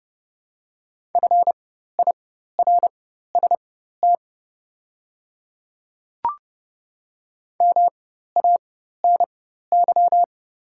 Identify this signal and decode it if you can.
Morse code, heard as a keyed tone, about 30 wpm, spaced as for 14 wpm, 690 Hz. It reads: FIRST MANY
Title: 4x Character Spacing - Top 100 Words - 30wpm